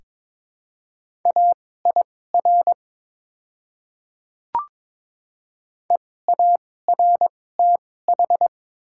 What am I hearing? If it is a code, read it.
Morse code, heard as a keyed tone, about 22 wpm, spaced as for 16 wpm, 700 Hz. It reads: AIR EARTH
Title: 2x Character Spacing - Top 200 Words - 22wpm